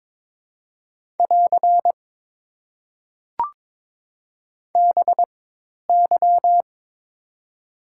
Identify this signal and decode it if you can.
Morse code, heard as a keyed tone, about 22 wpm, spaced as for 10 wpm, 690 Hz. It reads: <AR> BY
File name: Top 90 QSO Elements - 4x Character Spacing - 22wpm